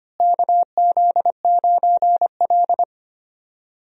Morse code, heard as a keyed tone, about 25 wpm, 700 Hz. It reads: KZ9L
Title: USA Call Signs - 4 Characters - Repeated 2x - 25wpm